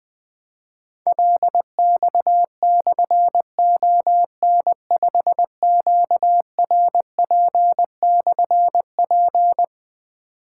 Morse code, heard as a keyed tone, about 20 wpm, 700 Hz. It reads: LX/ON5QRP/P
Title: Difficult Call Signs 20wpm